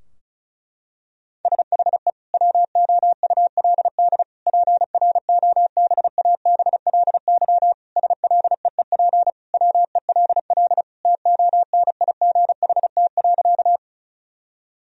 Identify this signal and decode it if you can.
Morse code, heard as a keyed tone, about 34 wpm, 700 Hz. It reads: SHE WOULD PROBABLY SLEEP WELL TONIGHT.
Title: Sentences from Top 1000 Words - Review Part 4 - 35wpm